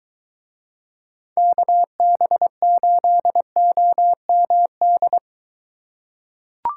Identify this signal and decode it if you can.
Morse code, heard as a keyed tone, about 21 wpm, 700 Hz. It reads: KB8OMD